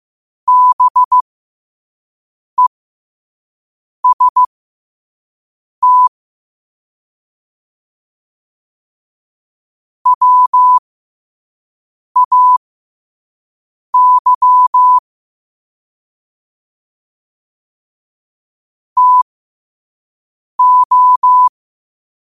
Morse code, heard as a keyed tone, about 15 wpm, spaced as for 5 wpm, 1000 Hz. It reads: BEST WAY TO
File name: qotdfiles / 5wpm